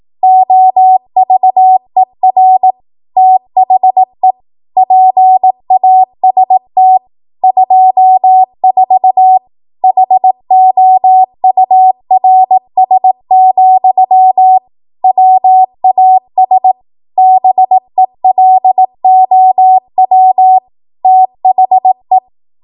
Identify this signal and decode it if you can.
Morse code, heard as a keyed tone, about 18 wpm, 750 Hz. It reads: OVER THE PAST 24 HOURS, WAS BELOW THE
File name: Morse / 18WPM